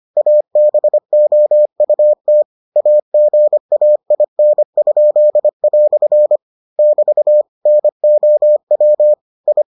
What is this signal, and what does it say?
Morse code, heard as a keyed tone, about 25 wpm, 600 Hz. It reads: ABOUT AGAIN?" <BT> NOW I